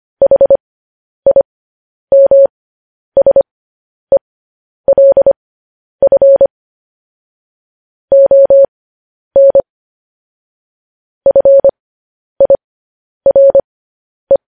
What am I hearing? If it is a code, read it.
Morse code, heard as a keyed tone, about 25 wpm, spaced as for 10 wpm, 550 Hz. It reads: HIMSELF ON FIRE
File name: podcast / News headlines-1x-25x10